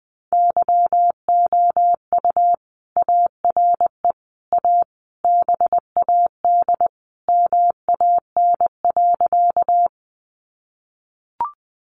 Morse code, heard as a keyed tone, about 20 wpm, 700 Hz. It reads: YOU ARE A BAD MAN.
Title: Sentences from Top 700 Words 20wpm